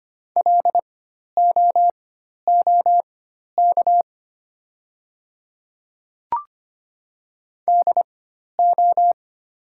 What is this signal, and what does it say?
Morse code, heard as a keyed tone, about 25 wpm, spaced as for 12 wpm, 700 Hz. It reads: LOOK DO
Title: 4x Character Spacing - Top 100 Words - 25wpm